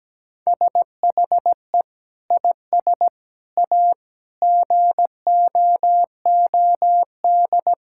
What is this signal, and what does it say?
Morse code, heard as a keyed tone, about 17 wpm, 700 Hz. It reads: SHE IS A GOOD
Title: Sentences from Top 1600 Words 17wpm